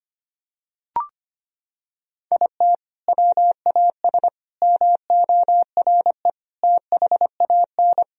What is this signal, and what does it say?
Morse code, heard as a keyed tone, about 25 wpm, 700 Hz. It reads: IT WAS MORE THAN
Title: Sentences from Top 1000 Words - Review Part 3 - 25wpm